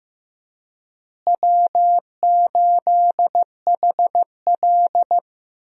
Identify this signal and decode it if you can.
Morse code, heard as a keyed tone, about 15 wpm, 700 Hz. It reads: W8HL
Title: USA Call Signs - 4 Characters - Repeated 2x - 15wpm